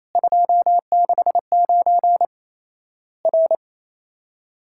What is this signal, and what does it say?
Morse code, heard as a keyed tone, about 28 wpm, 700 Hz. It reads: 269 R
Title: Worked All Europe DX Contest - QTC Exchange - 28wpm